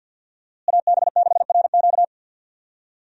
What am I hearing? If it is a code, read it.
Morse code, heard as a keyed tone, about 48 wpm, 700 Hz. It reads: AB6RQ
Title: USA Call Signs - 5 Characters - Repeated 3x - 50wpm